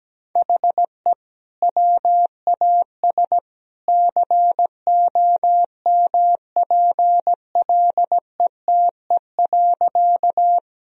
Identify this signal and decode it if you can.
Morse code, heard as a keyed tone, about 17 wpm, 700 Hz. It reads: HE WAS COMPLETE.
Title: Sentences from Top 1000 Words - Review Part 2 - 17wpm